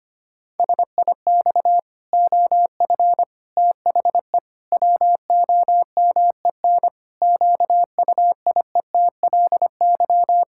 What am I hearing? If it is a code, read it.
Morse code, heard as a keyed tone, about 25 wpm, 700 Hz. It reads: SIX OF THE WOMEN QUIETLY
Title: Pangram Challenge v2 - 25wpm